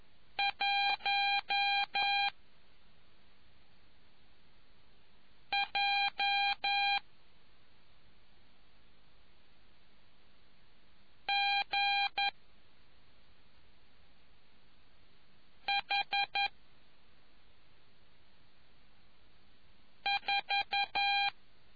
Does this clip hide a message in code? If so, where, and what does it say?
Yes: Morse code from 0.4 to 21.3 s, heard as a keyed tone, about 11 wpm, 800 Hz. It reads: 1 J G H 4